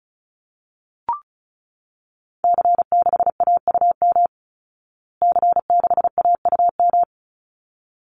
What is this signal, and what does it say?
Morse code, heard as a keyed tone, about 34 wpm, 700 Hz. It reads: C6AUM C6AUM
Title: Call Signs Repeated 3x - 35wpm